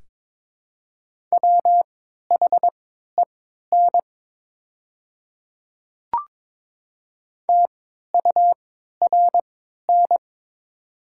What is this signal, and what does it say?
Morse code, heard as a keyed tone, about 22 wpm, spaced as for 12 wpm, 700 Hz. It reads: WHEN TURN
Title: 3x Character Spacing - Top 200 Words - 22wpm